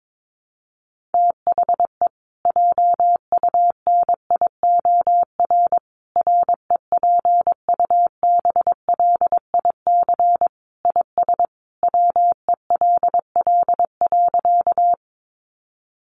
Morse code, heard as a keyed tone, about 22 wpm, 700 Hz. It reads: THE JUNIOR REPUBLIC IS WELL.
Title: Sentences from Top 1500 Words 22wpm